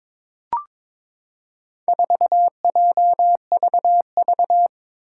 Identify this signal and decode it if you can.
Morse code, heard as a keyed tone, about 22 wpm, 700 Hz. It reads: 4JVV
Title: Four Challenging Letter-Number Combinations 22wpm